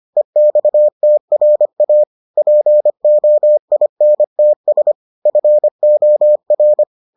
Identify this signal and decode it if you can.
Morse code, heard as a keyed tone, about 25 wpm, 600 Hz. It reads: EXTRA POINTS FOR